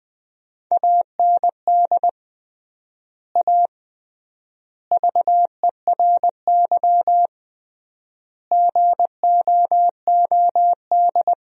Morse code, heard as a keyed tone, about 20 wpm, 700 Hz. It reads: AND A VERY GOOD
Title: 3x Word-Spacing - Sentences from Top 200 Words - 20wpm